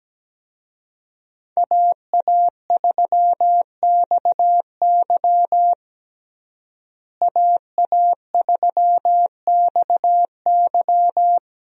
Morse code, heard as a keyed tone, about 17 wpm, 700 Hz. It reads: AA3XY AA3XY
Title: USA Call Signs - 5 Characters - Repeated 2x - 17wpm